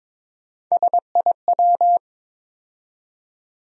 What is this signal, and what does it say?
Morse code, heard as a keyed tone, about 22 wpm, 700 Hz. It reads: SIW
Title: USA Call Sign Postfixes - 3 Letters 22wpm